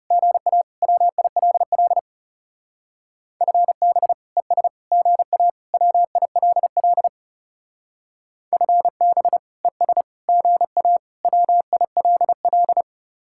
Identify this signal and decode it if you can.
Morse code, heard as a keyed tone, about 38 wpm, 700 Hz. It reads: GA WILL FB ES GA WILL FB ES GA WILL